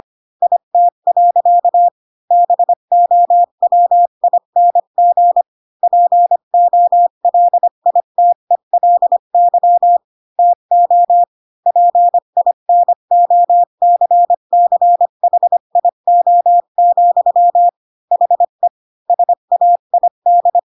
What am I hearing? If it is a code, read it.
Morse code, heard as a keyed tone, about 25 wpm, 700 Hz. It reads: IT. BOWING POLITELY TO PINOCCHIO, HE SAID
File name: Pinocchio / Pinocchio ch5 20 25 ik1hge